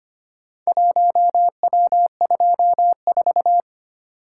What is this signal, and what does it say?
Morse code, heard as a keyed tone, about 25 wpm, 700 Hz. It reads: 1W24
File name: practice / 25wpm / Four Letter-Number Combinations - 25wpm